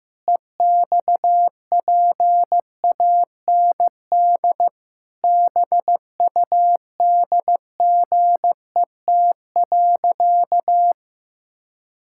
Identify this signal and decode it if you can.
Morse code, heard as a keyed tone, about 14 wpm, 700 Hz. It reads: EXPAND BUDGET.
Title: Pangram Challenge v2 - 15wpm